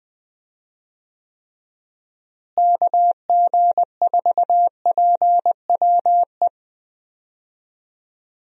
Morse code, heard as a keyed tone, about 20 wpm, 700 Hz. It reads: KG4PWE
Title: USA Call Signs - 6 Characters - Repeated 3x - 20wpm